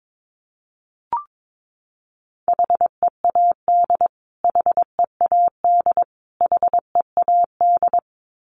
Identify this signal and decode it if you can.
Morse code, heard as a keyed tone, about 22 wpm, 700 Hz. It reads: HEAD HEAD HEAD
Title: Top 500 Words Repeated 22wpm